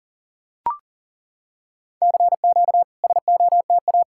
Morse code, heard as a keyed tone, about 40 wpm, 700 Hz. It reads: CQ SOTA
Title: SOTA QSO Practice 40wpm